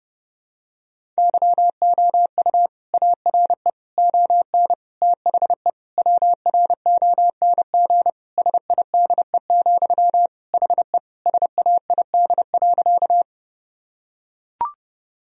Morse code, heard as a keyed tone, about 30 wpm, 700 Hz. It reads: YOU ARE ON THE WRONG SIDE, HE SAID.
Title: Sentences from Top 1000 Words - Review Part 1 - 30wpm